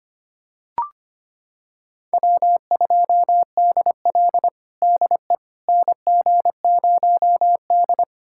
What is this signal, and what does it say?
Morse code, heard as a keyed tone, about 25 wpm, 700 Hz. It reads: W2DL DE NG0D
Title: USA Callsign DE Callsign - 4 Characters - 25wpm